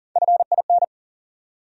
Morse code, heard as a keyed tone, about 40 wpm, 700 Hz. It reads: FIN
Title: USA Call Sign Postfixes - 3 Letters 40wpm